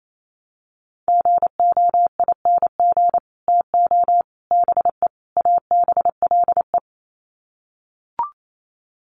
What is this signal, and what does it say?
Morse code, heard as a keyed tone, about 28 wpm, 700 Hz. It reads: GOING TO BE ABLE
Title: Sets of 4 Words 28wpm